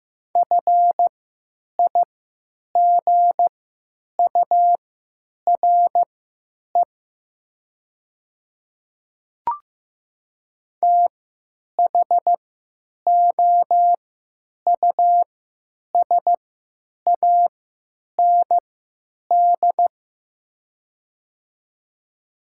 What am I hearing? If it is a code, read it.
Morse code, heard as a keyed tone, about 15 wpm, spaced as for 8 wpm, 700 Hz. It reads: FIGURE THOUSAND
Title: 3x Character Spacing - Top 500 Words - 15wpm